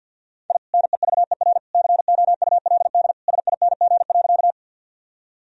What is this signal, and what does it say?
Morse code, heard as a keyed tone, about 50 wpm, 700 Hz. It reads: I NEVER COULD SING.
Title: Sentences from Top 1000 Words - Review Part 2 - 50wpm